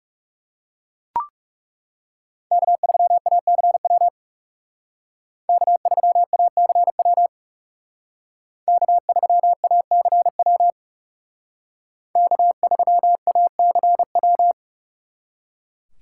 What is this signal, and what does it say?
Morse code, heard as a keyed tone, about 38 wpm, 700 Hz. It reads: K3ACW K3ACW K3ACW K3ACW